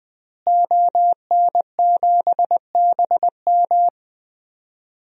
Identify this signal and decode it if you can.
Morse code, heard as a keyed tone, about 20 wpm, 700 Hz. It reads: ON7BM
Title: DXCC-100 Call Signs - 5 Characters - Repeated 3x - 20wpm